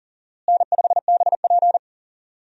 Morse code, heard as a keyed tone, about 40 wpm, 700 Hz. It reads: N5BP
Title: USA Call Signs - 4 Characters - Repeated 3x - 40wpm